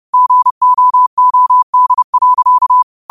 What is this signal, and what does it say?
Morse code, heard as a keyed tone, about 30 wpm, 1000 Hz. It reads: GOOD.
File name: qotdfiles / 30wpm